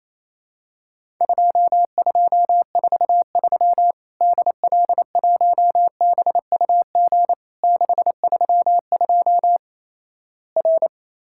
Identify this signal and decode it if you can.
Morse code, heard as a keyed tone, about 28 wpm, 700 Hz. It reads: 2243 DL1BUG 632 R
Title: Worked All Europe DX Contest - QTC Exchange - 28wpm